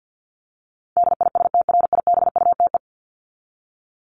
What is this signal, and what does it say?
Morse code, heard as a keyed tone, about 48 wpm, 700 Hz. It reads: DISTRIBUTE
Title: Top 1501-1750 Words 50wpm